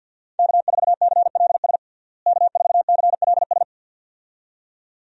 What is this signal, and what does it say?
Morse code, heard as a keyed tone, about 52 wpm, 700 Hz. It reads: K4CLS K4CLS
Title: USA Call Signs - 5 Characters - Repeated 2x - 50wpm